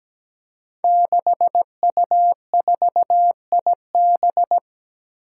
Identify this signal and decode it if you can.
Morse code, heard as a keyed tone, about 17 wpm, 700 Hz. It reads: 6U4IB
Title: Five Challenging Letter-Number Combinations 17wpm